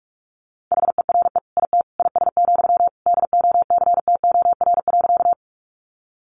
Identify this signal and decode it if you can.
Morse code, heard as a keyed tone, about 44 wpm, 700 Hz. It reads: HERE IT IS, DOCTOR.
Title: Sentences from Top 1500 Words 45wpm